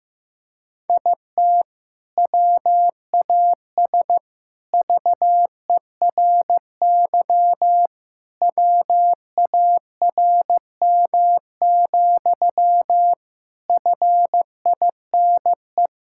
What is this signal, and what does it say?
Morse code, heard as a keyed tone, about 15 wpm, 700 Hz. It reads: IT WAS VERY WARM, FINE
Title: Sentences from Top 1000 Words - Review Part 5 - 15wpm